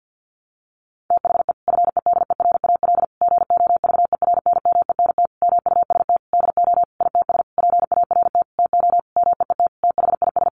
Morse code, heard as a keyed tone, about 49 wpm, 700 Hz. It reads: THE FEDERAL GOVERNMENT MUST DO ITS PART TO MEET THIS
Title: Sentences from Top 1100 Words 50wpm